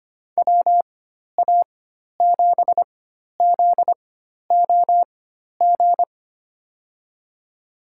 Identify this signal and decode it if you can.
Morse code, heard as a keyed tone, about 25 wpm, spaced as for 12 wpm, 700 Hz. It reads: WA7ZOG
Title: USA Call Signs - 6 Characters - 4x Character Spacing - 25wpm